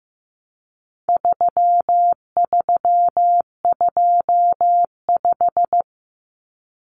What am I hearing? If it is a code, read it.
Morse code, heard as a keyed tone, about 15 wpm, 700 Hz. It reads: 3325
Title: Common Cut Numbers - 4-Digit 15wpm